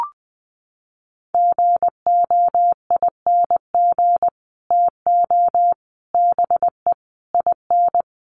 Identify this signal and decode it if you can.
Morse code, heard as a keyed tone, about 20 wpm, 700 Hz. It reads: GOING TO BE IN
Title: Sets of 5 Words Encore 20wpm